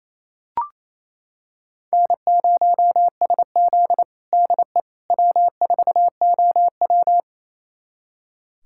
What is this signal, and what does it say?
Morse code, heard as a keyed tone, about 28 wpm, 700 Hz. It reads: N0SZ DE W4OW
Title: USA Callsign DE Callsign - 4 Characters - 28wpm